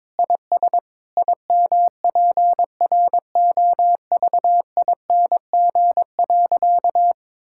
Morse code, heard as a keyed tone, about 22 wpm, 700 Hz. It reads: IS IMPROVING.
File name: practice / 22wpm / Sentences with 8-Word Phrases - 22wpm